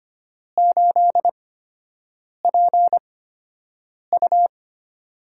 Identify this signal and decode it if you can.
Morse code, heard as a keyed tone, about 25 wpm, spaced as for 7 wpm, 700 Hz. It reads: 8PU
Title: USA Call Signs - 4 Characters - 8x Character Spacing - 25wpm